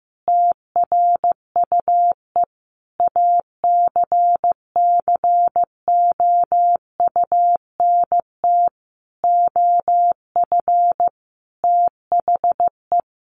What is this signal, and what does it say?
Morse code, heard as a keyed tone, about 14 wpm, 700 Hz. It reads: TRUE ACCOUNT OF THE
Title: Sentences from Top 900 Words 15wpm